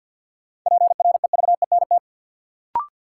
Morse code, heard as a keyed tone, about 46 wpm, 700 Hz. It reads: PREVENT
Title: Sentences from Top 1100 Words - New Word Drill Alt - 50wpm